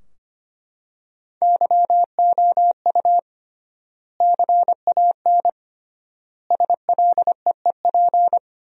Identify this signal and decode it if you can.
Morse code, heard as a keyed tone, about 25 wpm, 700 Hz. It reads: YOU CAN SLEEP
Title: 3x Word-Spacing - Sentences from Top 500 Words - 25wpm